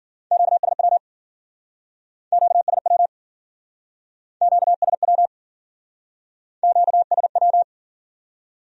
Morse code, heard as a keyed tone, about 50 wpm, 700 Hz. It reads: QSW QSW QSW QSW
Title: Speed-Racing Q-Codes - 40wpm